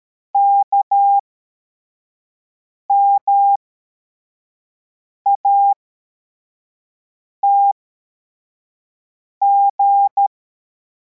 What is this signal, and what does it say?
Morse code, heard as a keyed tone, about 13 wpm, spaced as for 4 wpm, 800 Hz. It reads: KMATG